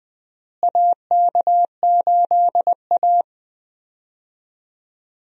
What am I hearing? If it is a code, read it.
Morse code, heard as a keyed tone, about 20 wpm, 700 Hz. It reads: AK8A
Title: USA Call Signs - 4 Characters - Repeated 3x - 20wpm